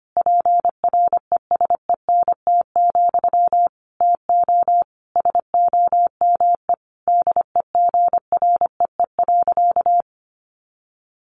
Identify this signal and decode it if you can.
Morse code, heard as a keyed tone, about 25 wpm, 700 Hz. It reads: PRESENT, TO SOME DEGREE.
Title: Sentences from Top 1500 Words 25wpm